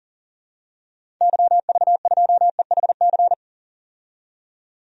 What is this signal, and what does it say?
Morse code, heard as a keyed tone, about 40 wpm, 700 Hz. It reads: YV2EHC
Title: DXCC-100 Call Signs - 6 Characters - 40wpm